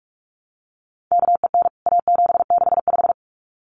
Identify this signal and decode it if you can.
Morse code, heard as a keyed tone, about 44 wpm, 700 Hz. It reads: KEN A765
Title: CWT Exchange 45wpm